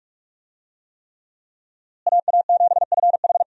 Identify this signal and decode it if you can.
Morse code, heard as a keyed tone, about 46 wpm, 700 Hz. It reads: AA7FH
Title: USA Call Signs - 5 Characters - Repeated 3x - 45wpm